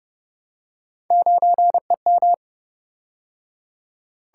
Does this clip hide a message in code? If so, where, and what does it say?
Yes: Morse code from 1.1 to 2.3 s, heard as a keyed tone, about 29 wpm, 700 Hz. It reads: 9EM